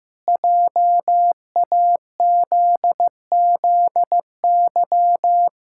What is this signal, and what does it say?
Morse code, heard as a keyed tone, about 15 wpm, 700 Hz. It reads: JAZZY
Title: Pangram Challenge v2 - 15wpm